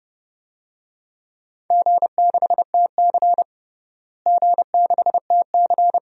Morse code, heard as a keyed tone, about 32 wpm, 700 Hz. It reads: G6TC G6TC
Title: DXCC-100 Call Signs - 4 Characters - Repeated 2x - 30wpm